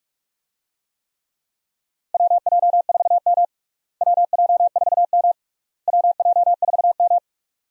Morse code, heard as a keyed tone, about 46 wpm, 700 Hz. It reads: WJ4M WJ4M WJ4M
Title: USA Call Signs - 4 Characters - Repeated 3x - 45wpm